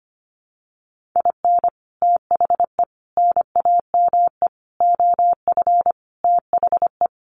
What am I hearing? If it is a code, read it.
Morse code, heard as a keyed tone, about 25 wpm, 700 Hz. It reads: IN THE NAME OF THE
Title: Sentences from Top 1500 Words 25wpm